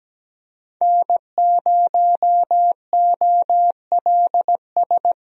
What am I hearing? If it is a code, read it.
Morse code, heard as a keyed tone, about 17 wpm, 700 Hz. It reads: N0OLS
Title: USA Call Signs - 5 Characters - 17wpm